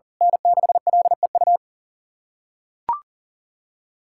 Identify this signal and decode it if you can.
Morse code, heard as a keyed tone, about 37 wpm, 700 Hz. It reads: N6LEU
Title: USA Callsign DE Callsign - 5 Characters - 40wpm